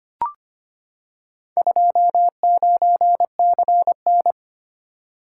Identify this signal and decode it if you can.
Morse code, heard as a keyed tone, about 25 wpm, 700 Hz. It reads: 29CN
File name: practice / 25wpm / Four Letter-Number Combinations - 25wpm